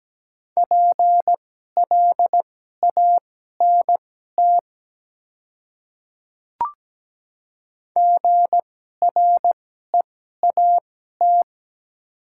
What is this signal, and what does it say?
Morse code, heard as a keyed tone, about 17 wpm, spaced as for 12 wpm, 700 Hz. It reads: PLANT GREAT
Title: 2x Character Spacing - Top 300 Words - 17wpm